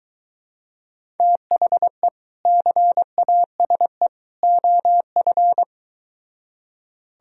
Morse code, heard as a keyed tone, about 23 wpm, 700 Hz. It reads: THE CASE OF